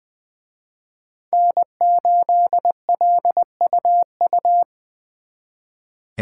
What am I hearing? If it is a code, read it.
Morse code, heard as a keyed tone, about 25 wpm, 700 Hz. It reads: N8LUU
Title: Speed Racing - USA 1x3 Call Signs 20wpm